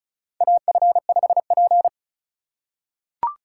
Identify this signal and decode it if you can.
Morse code, heard as a keyed tone, about 36 wpm, 700 Hz. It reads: AF5P
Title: K1USN SST QSO Practice - 35wpm